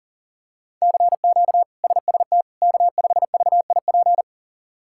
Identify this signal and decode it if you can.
Morse code, heard as a keyed tone, about 40 wpm, 700 Hz. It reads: CQ SST K5VIP
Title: K1USN SST QSO Practice - 40wpm